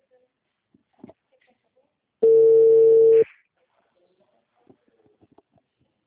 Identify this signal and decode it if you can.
Ringback tone (China, 450 Hz).
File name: out- -2003- - - .4162